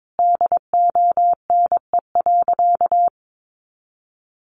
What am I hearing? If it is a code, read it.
Morse code, heard as a keyed tone, about 22 wpm, 700 Hz. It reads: DONE.
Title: Sentences from Top 500 Words 22wpm